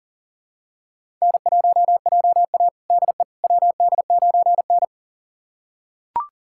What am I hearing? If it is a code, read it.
Morse code, heard as a keyed tone, about 40 wpm, 700 Hz. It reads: N1JA DE WD9N
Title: USA Callsign DE Callsign - 4 Characters - 40wpm